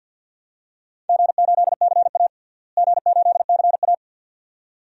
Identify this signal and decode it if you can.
Morse code, heard as a keyed tone, about 51 wpm, 700 Hz. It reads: G8CA G8CA
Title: DXCC-100 Call Signs - 4 Characters - Repeated 2x - 50wpm